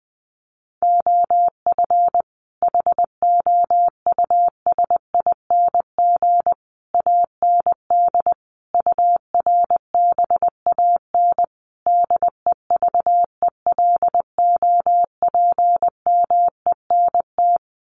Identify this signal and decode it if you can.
Morse code, heard as a keyed tone, about 20 wpm, 700 Hz. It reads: OF HOUSING AND URBAN DEVELOPMENT
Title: Sets of 5 Words Encore2 20wpm